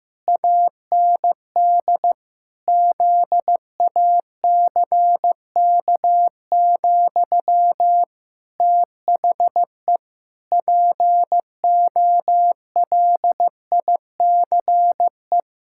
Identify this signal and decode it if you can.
Morse code, heard as a keyed tone, about 14 wpm, 700 Hz. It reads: AND ZACK, THE POLICE
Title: Pangram Challenge v2 - 15wpm